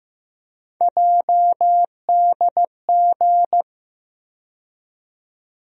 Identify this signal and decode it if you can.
Morse code, heard as a keyed tone, about 15 wpm, 700 Hz. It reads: JDG
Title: USA Call Sign Postfixes - 3 Letters 15wpm